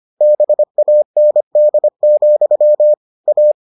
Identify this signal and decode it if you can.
Morse code, heard as a keyed tone, about 25 wpm, 600 Hz. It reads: BAND, A